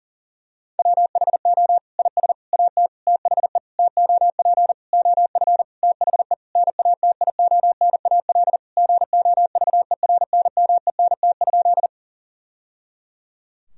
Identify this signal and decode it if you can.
Morse code, heard as a keyed tone, about 40 wpm, 700 Hz. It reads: WHO IS AT THE TOP OF THE NATIONAL GOVERNMENT?
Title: Sentences from Top 1000 Words - Review Part 4 - 40wpm